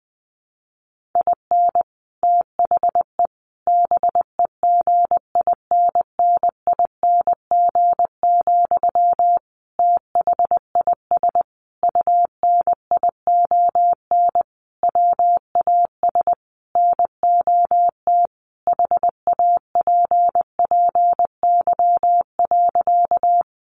Morse code, heard as a keyed tone, about 19 wpm, 700 Hz. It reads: IN THE BEGINNING, THIS UNION WAS NOT HAPPY.
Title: Sentences from Top 1400 Words 20wpm